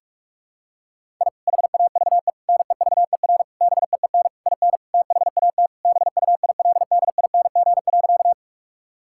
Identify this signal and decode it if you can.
Morse code, heard as a keyed tone, about 42 wpm, 700 Hz. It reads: I HAVE NEVER BEEN IN THAT BUILDING.
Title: Sentences from Top 1000 Words - Review Part 5 - 45wpm